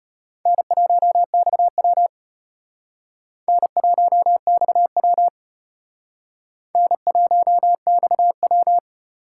Morse code, heard as a keyed tone, about 32 wpm, 700 Hz. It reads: N1XW N1XW N1XW